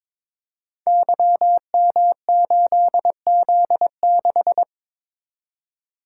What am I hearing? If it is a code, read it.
Morse code, heard as a keyed tone, about 22 wpm, 700 Hz. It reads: YM8Z6
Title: Code Groups - 5 Letter:Numbers - 22wpm